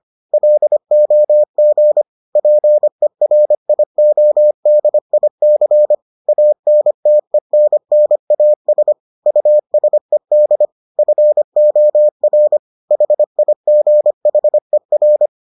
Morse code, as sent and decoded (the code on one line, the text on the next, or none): .-.. --- --. / .--. .-. .. --- -.. .. -.-. / .- -. - . -. -. .- ... / ..- ... -.. / ..-. --- .-. / .... .. --. .... .-.
LOG PERIODIC ANTENNAS USED FOR HIGHER